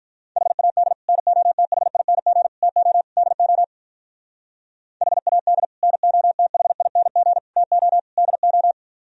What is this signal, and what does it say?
Morse code, heard as a keyed tone, about 44 wpm, 700 Hz. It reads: HAD NOTHING TO DO HAD NOTHING TO DO
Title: Speed-Racing - Sets of 4 Words 35wpm